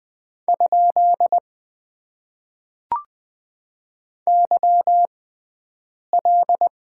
Morse code, heard as a keyed tone, about 20 wpm, spaced as for 7 wpm, 700 Hz. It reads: ? YL
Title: Top 90 QSO Elements - 6x Character Spacing - 20wpm